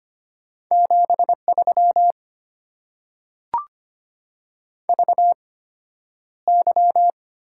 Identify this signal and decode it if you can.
Morse code, heard as a keyed tone, about 25 wpm, spaced as for 11 wpm, 700 Hz. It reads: 73 VY
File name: Top 90 QSO Elements - 8x Character Spacing - 25wpm